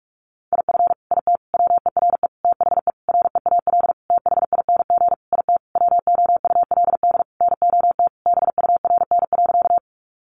Morse code, heard as a keyed tone, about 44 wpm, 700 Hz. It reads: IF IT WERE THE REAL THING IT WOULD NOT BURN.
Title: Sentences from Top 1300 Words 45wpm